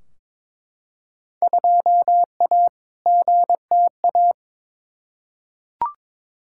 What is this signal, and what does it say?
Morse code, heard as a keyed tone, about 22 wpm, 700 Hz. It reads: 2A GTA
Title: Field Day Exchanges Encore - 22wpm